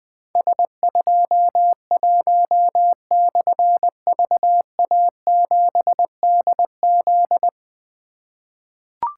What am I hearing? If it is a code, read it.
Morse code, heard as a keyed tone, about 20 wpm, 700 Hz. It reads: S21/VA7DZ
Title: Difficult Call Signs 20wpm